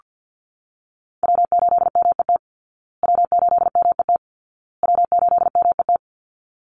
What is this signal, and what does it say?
Morse code, heard as a keyed tone, about 48 wpm, 700 Hz. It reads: W8MET W8MET W8MET
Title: Call Signs Repeated 2x - 50wpm